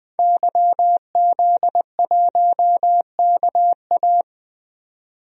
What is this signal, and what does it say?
Morse code, heard as a keyed tone, about 20 wpm, 700 Hz. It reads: YZ1KA
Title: DXCC-100 Call Signs - 5 Characters - 20wpm